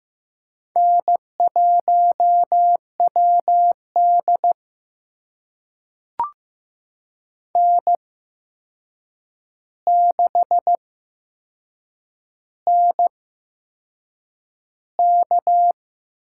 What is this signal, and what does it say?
Morse code, heard as a keyed tone, about 15 wpm, spaced as for 6 wpm, 700 Hz. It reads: N1WD N6NK
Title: USA Call Signs - 4 Characters - 8x Character Spacing - 15wpm